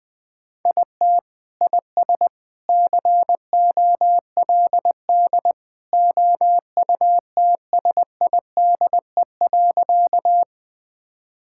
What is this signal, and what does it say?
Morse code, heard as a keyed tone, about 21 wpm, 700 Hz. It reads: IT IS COLD OUTSIDE.
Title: Sentences from Top 1000 Words - Review Part 6 - 20wpm